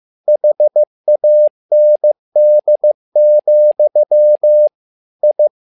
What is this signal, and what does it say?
Morse code, heard as a keyed tone, about 15 wpm, 600 Hz. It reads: HAND, I
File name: learning center / tower trouble / tower trouble 15wpm